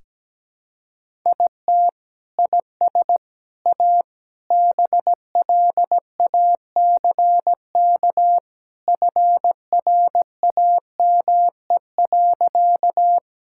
Morse code, heard as a keyed tone, about 17 wpm, 700 Hz. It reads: IT IS A BLACK FRAME.
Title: Sentences from Top 1000 Words - Review Part 6 - 17wpm